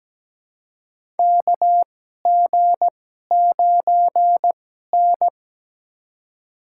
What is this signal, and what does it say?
Morse code, heard as a keyed tone, about 17 wpm, spaced as for 12 wpm, 700 Hz. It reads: KG9N
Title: USA Call Signs - 4 Characters - 2x Character Spacing - 17wpm